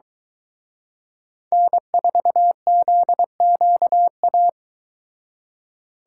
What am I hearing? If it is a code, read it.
Morse code, heard as a keyed tone, about 23 wpm, 700 Hz. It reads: N4ZQA